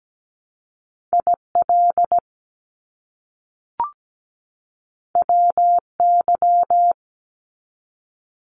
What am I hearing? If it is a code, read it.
Morse code, heard as a keyed tone, about 17 wpm, 700 Hz. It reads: IL WY
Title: US States 17wpm